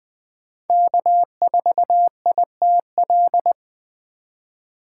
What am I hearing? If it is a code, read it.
Morse code, heard as a keyed tone, about 20 wpm, 700 Hz. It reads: K4ITL
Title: Speed Racing - USA 1x3 Call Signs 17wpm